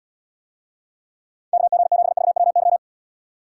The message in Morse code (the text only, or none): KG7VWY